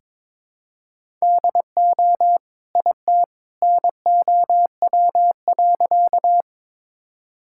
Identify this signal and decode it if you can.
Morse code, heard as a keyed tone, about 22 wpm, 700 Hz. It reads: DO IT NOW.
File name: practice / 22wpm / Sentences with 2-Word Phrases - 22wpm